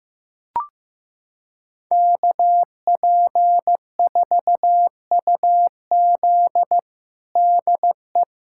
Morse code, heard as a keyed tone, about 15 wpm, 700 Hz. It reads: KP4UZ DE
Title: USA Callsign DE Callsign - 5 Characters - 15wpm